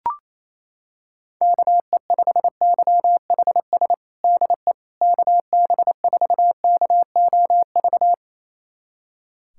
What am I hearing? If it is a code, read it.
Morse code, heard as a keyed tone, about 28 wpm, 700 Hz. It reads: KE5YHS DE KB4KOV
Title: USA Callsign DE Callsign - 6 Characters - 28wpm